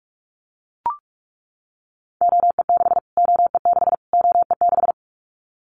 Morse code, heard as a keyed tone, about 44 wpm, 700 Hz. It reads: OE6 OE6 OE6
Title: Callsign Prefixes for Top 25 DXCC - Repeated 3x - 45wpm